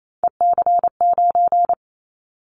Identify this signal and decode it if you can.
Morse code, heard as a keyed tone, about 28 wpm, 700 Hz. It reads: EC9
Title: Callsign Prefixes for Top 100 DXCC - 28wpm